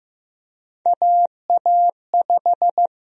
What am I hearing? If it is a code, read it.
Morse code, heard as a keyed tone, about 15 wpm, 700 Hz. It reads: AA5
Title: USA Call Sign Prefixes - Repeated 2x - 15wpm